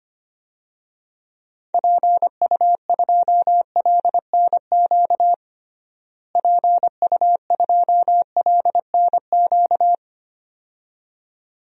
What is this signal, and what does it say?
Morse code, heard as a keyed tone, about 25 wpm, 700 Hz. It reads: PU2LNQ PU2LNQ
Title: DXCC-100 Call Signs - 6 Characters - Repeated 2x - 25wpm